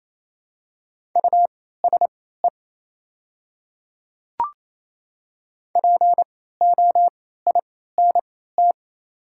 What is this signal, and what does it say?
Morse code, heard as a keyed tone, about 28 wpm, spaced as for 16 wpm, 700 Hz. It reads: USE POINT
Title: 3x Character Spacing - Top 200 Words - 28wpm